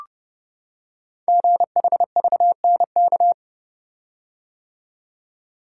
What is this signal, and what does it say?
Morse code, heard as a keyed tone, about 30 wpm, 690 Hz. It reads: GHVNK
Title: Code Groups - 5 Letters - 30wpm